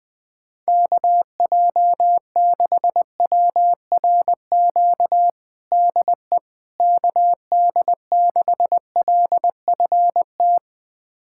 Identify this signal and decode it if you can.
Morse code, heard as a keyed tone, about 20 wpm, 700 Hz. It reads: KJ6WRQ DE KD6LFT
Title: USA Callsign DE Callsign - 6 Characters - 20wpm